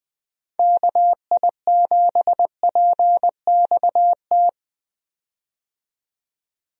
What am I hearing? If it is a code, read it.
Morse code, heard as a keyed tone, about 21 wpm, 700 Hz. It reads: KI7PXT
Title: USA Call Signs - 6 Characters - 20wpm